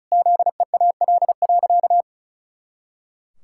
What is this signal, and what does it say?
Morse code, heard as a keyed tone, about 36 wpm, 700 Hz. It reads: ZEAL.
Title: Pangram Challenge v2 - 35wpm